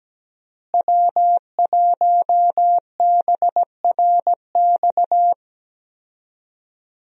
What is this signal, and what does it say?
Morse code, heard as a keyed tone, about 17 wpm, 700 Hz. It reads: W1BRX
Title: USA Call Signs - 5 Characters - Repeated 3x - 17wpm